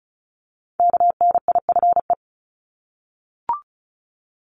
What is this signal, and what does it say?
Morse code, heard as a keyed tone, about 36 wpm, 700 Hz. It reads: KNIFE
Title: Multi-letter Phonemes 35wpm